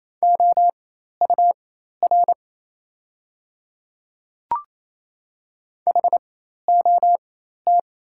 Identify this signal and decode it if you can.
Morse code, heard as a keyed tone, about 28 wpm, spaced as for 13 wpm, 690 Hz. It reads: OUR HOT
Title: 4x Character Spacing - Top 300 Words - 28wpm